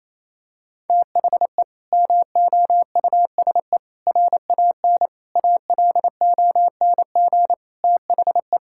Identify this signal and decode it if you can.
Morse code, heard as a keyed tone, about 28 wpm, 700 Hz. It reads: THE MOUSE RAN ALONG THE
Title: Sentences from Top 1000 Words - Review Part 5 - 28wpm